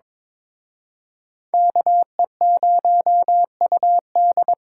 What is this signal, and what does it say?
Morse code, heard as a keyed tone, about 26 wpm, 700 Hz. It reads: KE0UD
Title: Speed Racing - USA 2x2 Call Signs 22wpm